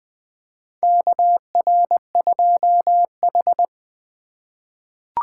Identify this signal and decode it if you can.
Morse code, heard as a keyed tone, about 20 wpm, 700 Hz. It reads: KR2H
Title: USA Call Signs - 4 Characters - 3x Character Spacing - 20wpm